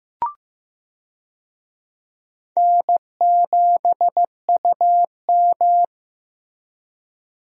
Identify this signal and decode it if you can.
Morse code, heard as a keyed tone, about 16 wpm, 700 Hz. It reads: N7UM
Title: USA Call Signs - 4 Characters - Repeated 3x - 15wpm